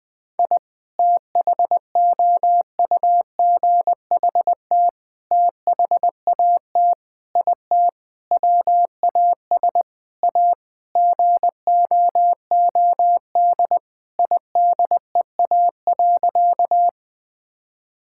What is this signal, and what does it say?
Morse code, heard as a keyed tone, about 20 wpm, 700 Hz. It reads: I THOUGHT THAT IT WAS A GOOD IDEA.
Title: Sentences with 4-Word Phrases - 20wpm